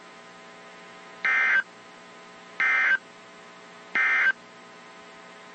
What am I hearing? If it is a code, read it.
Emergency Alert System (SAME) tones, heard as harsh buzzing bursts. It is heard from 1.1 to 4.4 s.
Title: WZ2523 - NOAA Weather Radio